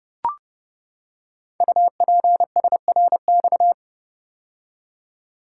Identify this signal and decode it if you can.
Morse code, heard as a keyed tone, about 30 wpm, 700 Hz. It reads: UPSRX
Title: Code Groups - 5 Letters - 30wpm